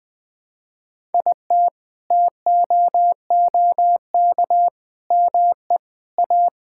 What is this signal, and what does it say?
Morse code, heard as a keyed tone, about 20 wpm, 700 Hz. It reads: IT TOOK ME A
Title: Sets of 6 Words - 20wpm